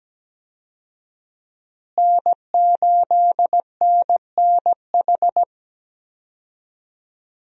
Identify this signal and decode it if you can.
Morse code, heard as a keyed tone, about 17 wpm, 700 Hz. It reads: N8NNH
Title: USA Call Signs - 5 Characters - Repeated 2x - 17wpm